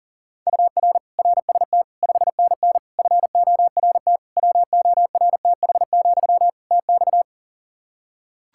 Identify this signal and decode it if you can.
Morse code, heard as a keyed tone, about 40 wpm, 700 Hz. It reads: UR RST 5NN FORT WORTH, TX
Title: Practice QSO - Exchange 1 - Beginning - 40wpm